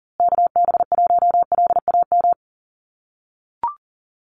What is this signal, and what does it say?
Morse code, heard as a keyed tone, about 40 wpm, 700 Hz. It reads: KB1LAM
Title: USA Call Signs Repeated 3x 40wpm